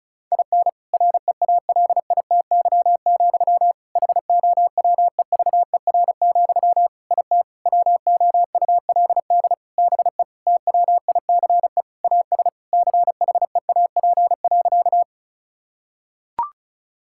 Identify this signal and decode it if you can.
Morse code, heard as a keyed tone, about 34 wpm, 700 Hz. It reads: IN REALITY, HOWEVER, IT WOULD BE TWICE AS CHEAP.
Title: Sentences from Top 1600 Words 35wpm